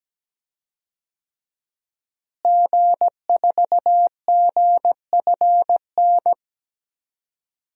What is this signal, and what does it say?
Morse code, heard as a keyed tone, about 17 wpm, 700 Hz. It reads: G4GFN
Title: DXCC-100 Call Signs - 5 Characters - Repeated 3x - 17wpm